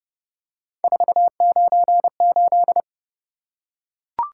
Morse code, heard as a keyed tone, about 27 wpm, 700 Hz. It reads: 498